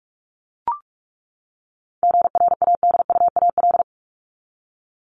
Morse code, heard as a keyed tone, about 44 wpm, 700 Hz. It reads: GRADUAL
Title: LY Suffix - Top 100 Words with Base Words 45wpm